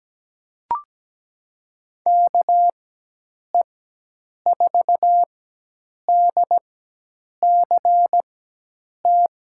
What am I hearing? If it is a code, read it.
Morse code, heard as a keyed tone, about 17 wpm, spaced as for 8 wpm, 700 Hz. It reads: KE4DCT
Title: USA Call Signs - 6 Characters - 4x Character Spacing - 17wpm